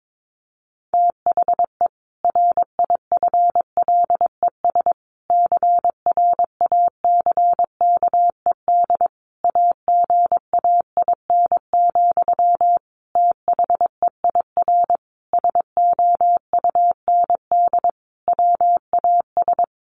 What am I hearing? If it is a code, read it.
Morse code, heard as a keyed tone, about 22 wpm, 700 Hz. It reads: THE RIFLES CRACKED AGAIN, THEIR SOUND WAS
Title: A Sound Of Thunder 22wpm